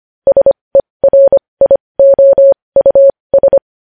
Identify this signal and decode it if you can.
Morse code, heard as a keyed tone, about 25 wpm, spaced as for 20 wpm, 550 Hz. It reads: SERIOUS
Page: News Headlines in Morse Code at 20 WPM